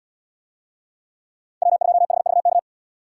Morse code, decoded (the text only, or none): W2HVK